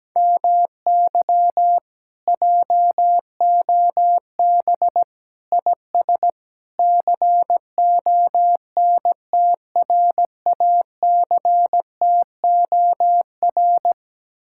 Morse code, heard as a keyed tone, about 17 wpm, 700 Hz. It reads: MY JOB IS CONTRACTOR
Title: Speed-Racing Common QSO Phrases - 17wpm